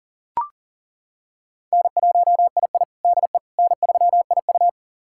Morse code, heard as a keyed tone, about 40 wpm, 700 Hz. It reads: N1II DE N3IU
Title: USA Callsign DE Callsign - 4 Characters - 40wpm